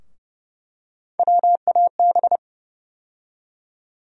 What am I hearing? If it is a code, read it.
Morse code, heard as a keyed tone, about 31 wpm, 700 Hz. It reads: WAB
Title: USA Call Sign Postfixes - 3 Letters 30wpm